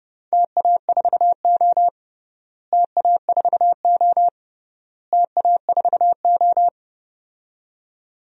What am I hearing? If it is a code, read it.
Morse code, heard as a keyed tone, about 32 wpm, 700 Hz. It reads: TA4O TA4O TA4O
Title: DXCC-100 Call Signs - 4 Characters - Repeated 3x - 30wpm